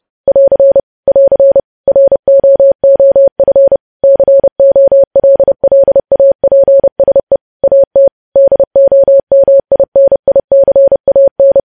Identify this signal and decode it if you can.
Morse code, heard as a keyed tone, about 30 wpm, 550 Hz. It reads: <AR> <AR> ROOF COLLAPSE AT DOMINICAN
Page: news headlines in morse code at 30 wpm